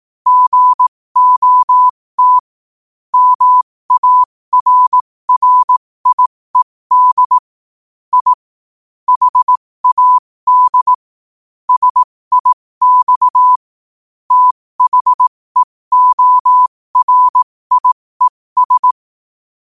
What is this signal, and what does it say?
Morse code, heard as a keyed tone, about 18 wpm, spaced as for 15 wpm, 1000 Hz. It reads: GOT MARRIED I HAD SIX THEORIES
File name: qotdfiles / 15wpm